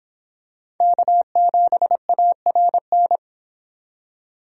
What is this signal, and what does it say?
Morse code, heard as a keyed tone, about 26 wpm, 700 Hz. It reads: K7ARN